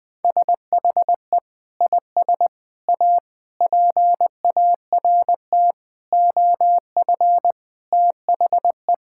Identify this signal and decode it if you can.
Morse code, heard as a keyed tone, about 20 wpm, 700 Hz. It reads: SHE IS A PART OF THE
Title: Sentences from Top 1000 Words - Review Part 5 - 20wpm